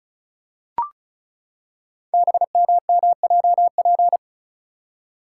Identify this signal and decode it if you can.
Morse code, heard as a keyed tone, about 34 wpm, 690 Hz. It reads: BMMJP
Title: Code Groups - 5 Letter:Numbers - 35wpm